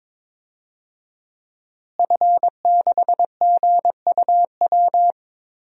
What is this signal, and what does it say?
Morse code, heard as a keyed tone, about 21 wpm, 700 Hz. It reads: F6GUW
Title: DXCC-100 Call Signs - 5 Characters - Repeated 2x - 22wpm